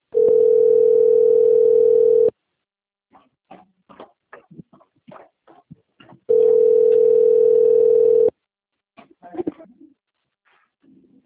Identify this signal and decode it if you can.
Ringback tone (North America), 440 + 480 Hz.